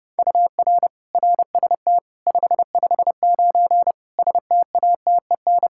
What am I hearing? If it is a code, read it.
Morse code, heard as a keyed tone, about 32 wpm, 700 Hz. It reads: UR RST 559 STATEN
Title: Practice QSO - Exchange 1 - Beginning - 30wpm